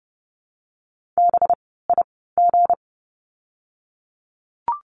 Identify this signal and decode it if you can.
Morse code, heard as a keyed tone, about 30 wpm, spaced as for 17 wpm, 700 Hz. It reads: BIG
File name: course / 30:10wpm / 230 Lesson 37f - Letter x - Words 30wpm